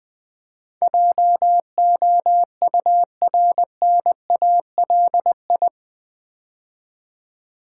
Morse code, heard as a keyed tone, about 19 wpm, 700 Hz. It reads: JOURNALI
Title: Build A Word - 4 Letters At A Time - 10-Letter Words - 20wpm